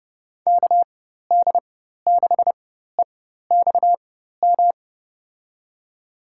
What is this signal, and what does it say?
Morse code, heard as a keyed tone, about 30 wpm, spaced as for 14 wpm, 700 Hz. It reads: KD6EXM
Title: USA Call Signs - 4x Character Spacing - 30wpm